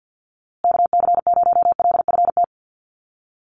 Morse code, heard as a keyed tone, about 50 wpm, 700 Hz. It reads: KC0LFT
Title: USA Call Signs Repeated 2x 50wpm